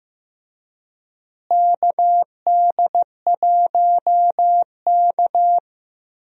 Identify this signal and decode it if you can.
Morse code, heard as a keyed tone, about 15 wpm, 700 Hz. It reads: KD1K
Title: USA Call Signs - 4 Characters - Repeated 3x - 15wpm